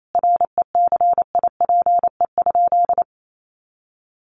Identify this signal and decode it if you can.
Morse code, heard as a keyed tone, about 28 wpm, 700 Hz. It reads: RECIPE?
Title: Sentences from Top 1200 Words 28wpm